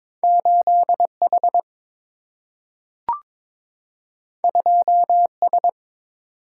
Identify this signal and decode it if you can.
Morse code, heard as a keyed tone, about 22 wpm, 700 Hz. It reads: 8 H 2 S
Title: Single-Letter Single-Number 22wpm